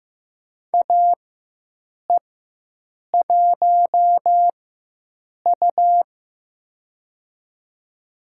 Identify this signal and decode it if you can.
Morse code, heard as a keyed tone, about 15 wpm, spaced as for 7 wpm, 700 Hz. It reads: AE1U
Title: USA Call Signs - 4x Character Spacing - 15wpm